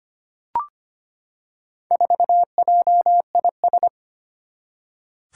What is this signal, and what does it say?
Morse code, heard as a keyed tone, about 25 wpm, 700 Hz. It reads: 4JIS
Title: Four Challenging Letter-Number Combinations 25wpm